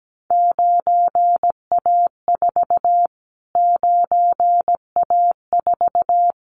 Morse code, heard as a keyed tone, about 17 wpm, 700 Hz. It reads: 9A4 9A4
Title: Callsign Prefixes for Top 25 DXCC - Repeated 3x - 17wpm